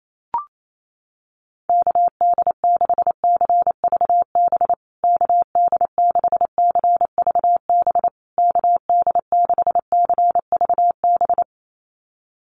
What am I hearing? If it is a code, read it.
Morse code, heard as a keyed tone, about 28 wpm, 700 Hz. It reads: KD6CVB KD6CVB KD6CVB
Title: USA Call Signs Repeated 3x 28wpm